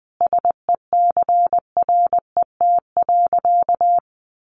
Morse code, heard as a keyed tone, about 20 wpm, 700 Hz. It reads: SECRET.
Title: Sentences from Top 1200 Words 20wpm